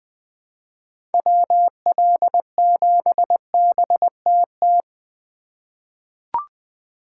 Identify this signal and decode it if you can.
Morse code, heard as a keyed tone, about 20 wpm, 700 Hz. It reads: WL7BTT
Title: USA Call Signs - 6 Characters - 2x Character Spacing - 20wpm